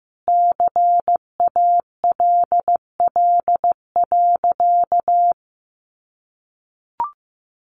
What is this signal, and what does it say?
Morse code, heard as a keyed tone, about 15 wpm, 700 Hz. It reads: CALL.
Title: Sentences from Top 900 Words 15wpm